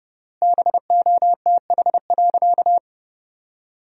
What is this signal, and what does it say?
Morse code, heard as a keyed tone, about 30 wpm, 700 Hz. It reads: BOTH.
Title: 3x Word-Spacing - Sentences from Top 300 Words - 30wpm